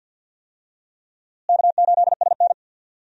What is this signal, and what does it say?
Morse code, heard as a keyed tone, about 52 wpm, 700 Hz. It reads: K8SN
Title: USA Call Signs - 4 Characters - Repeated 2x - 50wpm